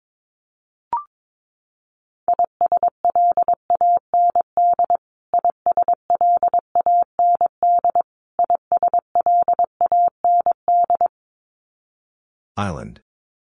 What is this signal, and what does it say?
Morse code, heard as a keyed tone, about 22 wpm, 700 Hz. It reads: ISLAND ISLAND ISLAND
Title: Top 500 Words Repeated 22wpm